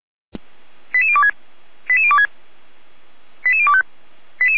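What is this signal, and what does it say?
Five-tone selective call (ZVEI), heard as quick runs of five tones, tones of about 70 ms. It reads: 88026 88026 88025
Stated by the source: ZVEI five-tone call 88383 received on SDR.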